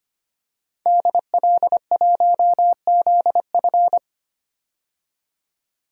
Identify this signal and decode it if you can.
Morse code, heard as a keyed tone, about 25 wpm, 700 Hz. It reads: DL1ZF
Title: DXCC-100 Call Signs - 5 Characters - 25wpm